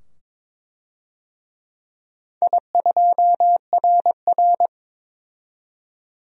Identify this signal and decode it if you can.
Morse code, heard as a keyed tone, about 22 wpm, 700 Hz. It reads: I2RR
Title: DXCC-100 Call Signs - 4 Characters - 22wpm